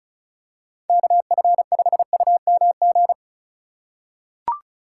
Morse code, heard as a keyed tone, about 36 wpm, 700 Hz. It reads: KF5UMG
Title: USA Call Signs - 6 Characters - 3x Character Spacing - 35wpm